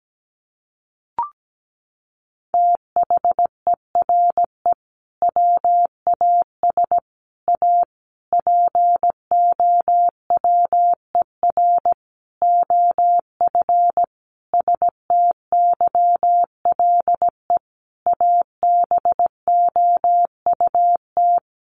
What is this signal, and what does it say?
Morse code, heard as a keyed tone, about 17 wpm, 700 Hz. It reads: THERE WAS A POWER OF STYLE ABOUT
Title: Sentences from Top 800 Words 17wpm